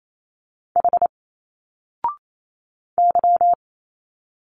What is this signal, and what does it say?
Morse code, heard as a keyed tone, about 28 wpm, 700 Hz. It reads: H Y
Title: Single Letters 28wpm